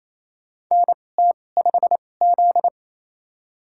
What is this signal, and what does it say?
Morse code, heard as a keyed tone, about 28 wpm, spaced as for 20 wpm, 700 Hz. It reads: NT5Z
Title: USA Call Signs - 4 Characters - 2x Character Spacing - 28wpm